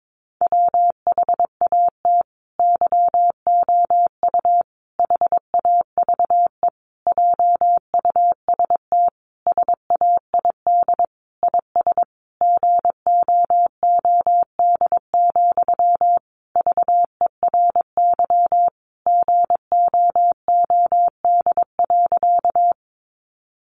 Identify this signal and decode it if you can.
Morse code, heard as a keyed tone, about 22 wpm, 700 Hz. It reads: WHAT YOU HAVE JUST SAID IS GOOD, VERY GOOD.
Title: Sentences from Top 200 Words 22wpm